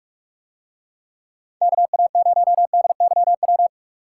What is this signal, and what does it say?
Morse code, heard as a keyed tone, about 45 wpm, 700 Hz. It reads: KA0DYW